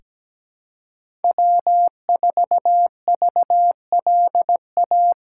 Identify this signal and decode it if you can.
Morse code, heard as a keyed tone, about 17 wpm, 700 Hz. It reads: W4VLA
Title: USA Call Signs - 5 Characters - 2x Character Spacing - 17wpm